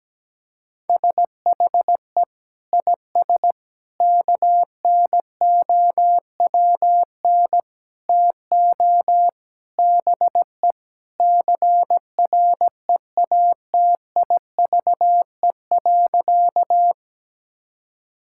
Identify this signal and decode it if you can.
Morse code, heard as a keyed tone, about 17 wpm, 700 Hz. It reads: SHE IS KNOWN TO BE CREATIVE.
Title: Sentences from Top 1000 Words - Review Part 5 - 17wpm